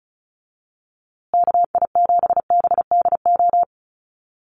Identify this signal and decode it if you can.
Morse code, heard as a keyed tone, about 34 wpm, 700 Hz. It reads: KI7BDO
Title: USA Call Signs Repeated 2x 35wpm